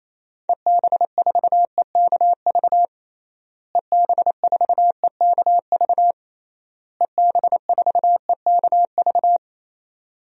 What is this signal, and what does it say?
Morse code, heard as a keyed tone, about 28 wpm, 700 Hz. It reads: EB4EKV EB4EKV EB4EKV
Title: DXCC-100 Call Signs - 6 Characters - Repeated 3x - 28wpm